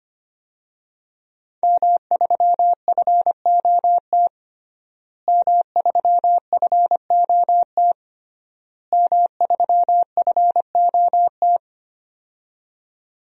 Morse code, heard as a keyed tone, about 25 wpm, 700 Hz. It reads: M3FOT M3FOT M3FOT
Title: DXCC-100 Call Signs - 5 Characters - Repeated 3x - 25wpm